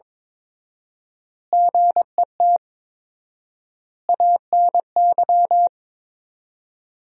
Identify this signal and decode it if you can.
Morse code, heard as a keyed tone, about 22 wpm, 700 Hz. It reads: GET ANY
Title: 4x Word-Spacing - Sentences from Top 400 Words - 22wpm